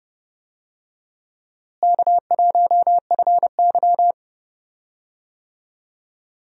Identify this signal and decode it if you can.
Morse code, heard as a keyed tone, about 31 wpm, 700 Hz. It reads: K1FY
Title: USA Call Signs - 4 Characters - Repeated 2x - 30wpm